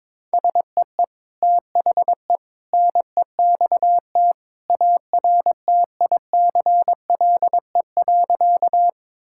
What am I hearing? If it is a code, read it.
Morse code, heard as a keyed tone, about 21 wpm, 700 Hz. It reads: SEE THE NEXT ARTICLE.
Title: Sentences from Top 1000 Words - Review Part 5 - 22wpm